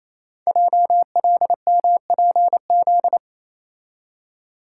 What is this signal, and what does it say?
Morse code, heard as a keyed tone, about 29 wpm, 700 Hz. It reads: JLMPZ
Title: Code Groups - 5 Letters - 28wpm